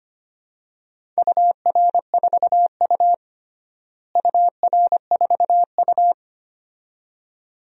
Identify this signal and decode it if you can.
Morse code, heard as a keyed tone, about 25 wpm, 700 Hz. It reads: UR4U UR4U
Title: DXCC-100 Call Signs - 4 Characters - Repeated 2x - 25wpm